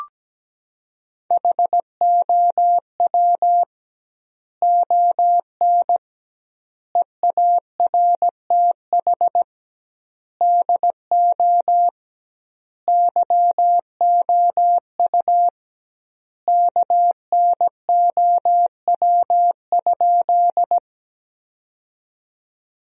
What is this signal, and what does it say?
Morse code, heard as a keyed tone, about 17 wpm, 700 Hz. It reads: HOW ON EARTH DO YOU KNOW?
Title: 2x Word-Spacing - Sentences from Top 200 Words - 17wpm